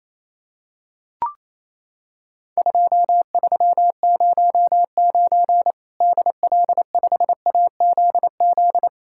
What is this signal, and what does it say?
Morse code, heard as a keyed tone, about 28 wpm, 700 Hz. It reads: 2309 DL5AZZ
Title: Worked All Europe DX Contest - QTC Exchange - 28wpm